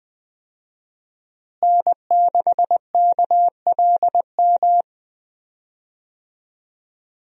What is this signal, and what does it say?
Morse code, heard as a keyed tone, about 21 wpm, 700 Hz. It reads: N6KLM
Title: USA Call Signs - 5 Characters - Repeated 2x - 20wpm